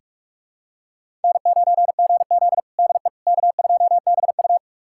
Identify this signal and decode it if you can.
Morse code, heard as a keyed tone, about 42 wpm, 700 Hz. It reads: N9GZ DE K2BU
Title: USA Callsign DE Callsign - 4 Characters - 45wpm